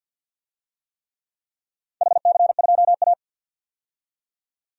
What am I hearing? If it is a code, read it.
Morse code, heard as a keyed tone, about 49 wpm, 700 Hz. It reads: HC2A
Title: DXCC-100 Call Signs - 4 Characters - 50wpm